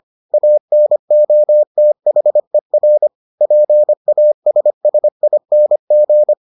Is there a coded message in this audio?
Yes: Morse code from 0.3 to 6.3 s, heard as a keyed tone, about 25 wpm, 600 Hz. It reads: ANOTHER PASSING